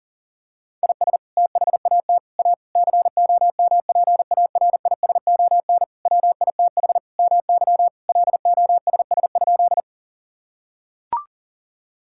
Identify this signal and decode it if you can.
Morse code, heard as a keyed tone, about 37 wpm, 700 Hz. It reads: IS THAT A COMPARISON WITH MY LOSS?
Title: Sentences from Top 1600 Words 40wpm